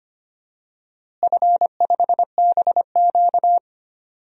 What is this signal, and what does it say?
Morse code, heard as a keyed tone, about 25 wpm, 700 Hz. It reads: F5BQ
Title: DXCC-100 Call Signs - 4 Characters - Repeated 2x - 25wpm